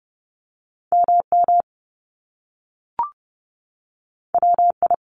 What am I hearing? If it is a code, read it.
Morse code, heard as a keyed tone, about 30 wpm, 700 Hz. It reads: MM WI
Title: Top Two Letter Combinations in Words 30wpm